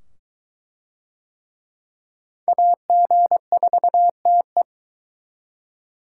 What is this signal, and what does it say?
Morse code, heard as a keyed tone, about 22 wpm, 700 Hz. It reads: AG4TE